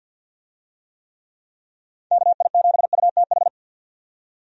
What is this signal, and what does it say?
Morse code, heard as a keyed tone, about 51 wpm, 700 Hz. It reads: KI7UTH
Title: USA Call Signs - 6 Characters - 50wpm